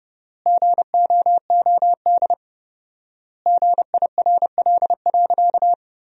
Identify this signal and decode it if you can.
Morse code, heard as a keyed tone, about 30 wpm, 700 Hz. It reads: GOOD GIRL.
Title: 4x Word-Spacing - Sentences from Top 400 Words - 30wpm